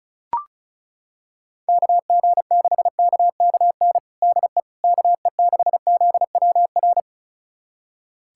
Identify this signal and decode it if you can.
Morse code, heard as a keyed tone, about 35 wpm, 700 Hz. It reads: KG6KKN DE KE6ZWR
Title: USA Callsign DE Callsign - 6 Characters - 35wpm